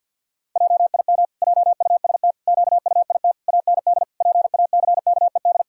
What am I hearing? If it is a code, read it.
Morse code, heard as a keyed tone, about 50 wpm, 700 Hz. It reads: JIM JUST QUIT AND PACKED
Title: Pangram Challenge v2 - 50wpm